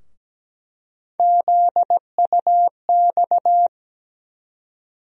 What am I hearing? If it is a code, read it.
Morse code, heard as a keyed tone, about 17 wpm, 700 Hz. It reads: ZUX
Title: USA Call Sign Postfixes - 3 Letters 17wpm